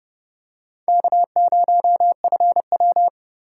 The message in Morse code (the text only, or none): K0FW